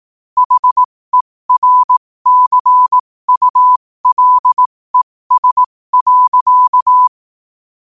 Morse code, heard as a keyed tone, about 18 wpm, spaced as for 15 wpm, 1000 Hz. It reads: HERCULES.
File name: qotdfiles / 15wpm